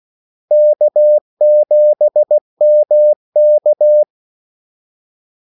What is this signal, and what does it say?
Morse code, heard as a keyed tone, about 16 wpm, 600 Hz. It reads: K7MK
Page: Kaldesignaler 16 wpm